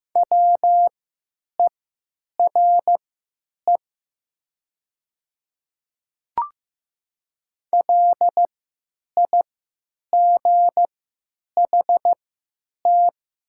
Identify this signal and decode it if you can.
Morse code, heard as a keyed tone, about 15 wpm, spaced as for 8 wpm, 700 Hz. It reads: WERE LIGHT
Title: 3x Character Spacing - Top 200 Words - 15wpm